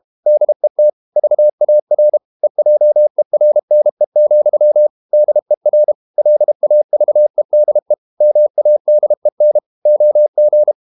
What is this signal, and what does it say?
Morse code, heard as a keyed tone, about 32 wpm, 600 Hz. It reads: DET VAR EJERNE, DER LAVEDE MADEN OG
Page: Ferie 32 wpm